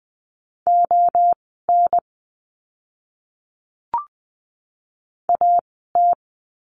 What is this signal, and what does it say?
Morse code, heard as a keyed tone, about 20 wpm, spaced as for 14 wpm, 700 Hz. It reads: ON AT
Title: 030 Lesson 6f - Letter i - Words 20wpm